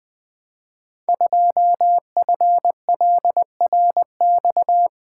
Morse code, heard as a keyed tone, about 20 wpm, 700 Hz. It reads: 2FLRX
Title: Code Groups - 5 Letter:Numbers - 20wpm